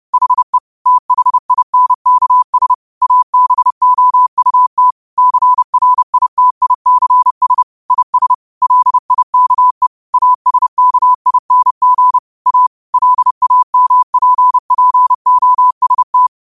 Morse code, heard as a keyed tone, about 30 wpm, 1000 Hz. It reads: HE THINKS ABOUT CRITICS IS LIKE ASKING A LAMPPOST
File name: qotdfiles / 30wpm